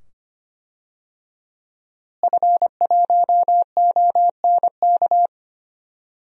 Morse code, heard as a keyed tone, about 25 wpm, 700 Hz. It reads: F1ONK
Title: DXCC-100 Call Signs - 5 Characters - Repeated 2x - 25wpm